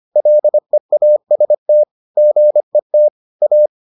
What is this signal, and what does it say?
Morse code, heard as a keyed tone, about 25 wpm, 600 Hz. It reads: LEAST GET A